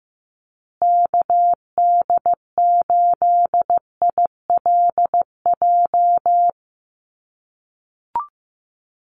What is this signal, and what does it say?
Morse code, heard as a keyed tone, about 15 wpm, 700 Hz. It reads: KD8ILJ
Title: USA Call Signs Repeated 3x 15wpm